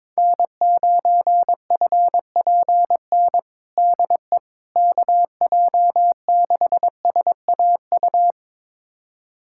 Morse code, heard as a keyed tone, about 22 wpm, 700 Hz. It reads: N9FPN DE KJ6SAU
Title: USA Callsign DE Callsign 22wpm